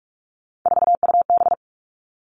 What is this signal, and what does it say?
Morse code, heard as a keyed tone, about 44 wpm, 700 Hz. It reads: 4UB
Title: All Cut Numbers - 3-Digit 45wpm